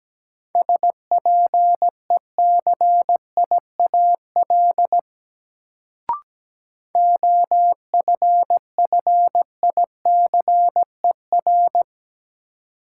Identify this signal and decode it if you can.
Morse code, heard as a keyed tone, about 17 wpm, 700 Hz. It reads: SPECIAL OFFICER
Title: Sentences from Top 1200 Words - New Word Drill Alt - 17wpm